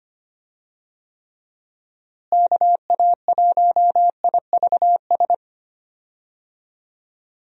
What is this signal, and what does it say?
Morse code, heard as a keyed tone, about 25 wpm, 700 Hz. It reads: KA1IVS
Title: USA Call Signs - 6 Characters - Repeated 2x - 25wpm